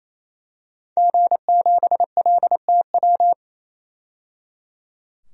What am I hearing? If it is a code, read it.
Morse code, heard as a keyed tone, about 28 wpm, 700 Hz. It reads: G7LTW
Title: DXCC-100 Call Signs - 5 Characters - 28wpm